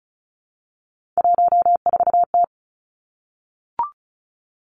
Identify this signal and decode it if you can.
Morse code, heard as a keyed tone, about 36 wpm, 700 Hz. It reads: 14T
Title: Three Challenging Letter-Number Combinations 35wpm